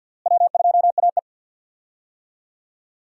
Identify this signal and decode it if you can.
Morse code, heard as a keyed tone, about 49 wpm, 700 Hz. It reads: W2AE
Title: USA Call Signs - 4 Characters - Repeated 2x - 50wpm